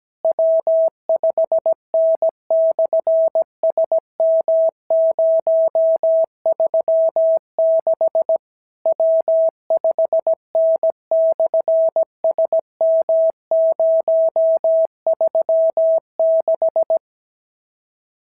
Morse code, heard as a keyed tone, about 17 wpm, 650 Hz. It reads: W5N/SM036 W5N/SM036
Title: SOTA QSO Practice 17wpm